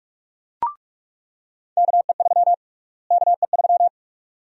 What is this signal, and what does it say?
Morse code, heard as a keyed tone, about 42 wpm, 700 Hz. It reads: KE3 KE3
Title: USA Call Sign Prefixes - Repeated 2x - 45wpm